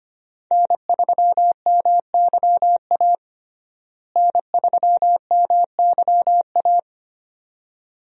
Morse code, heard as a keyed tone, about 25 wpm, 700 Hz. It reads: N3MYA N3MYA
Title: USA Call Signs - 5 Characters - Repeated 2x - 25wpm